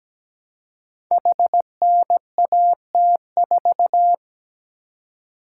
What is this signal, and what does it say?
Morse code, heard as a keyed tone, about 17 wpm, 700 Hz. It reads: HNAT4
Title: Code Groups - 5 Letter:Numbers - 17wpm